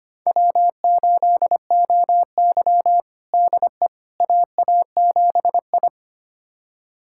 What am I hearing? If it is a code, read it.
Morse code, heard as a keyed tone, about 25 wpm, 700 Hz. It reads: W8OY DE AA7I
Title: USA Callsign DE Callsign - 4 Characters - 25wpm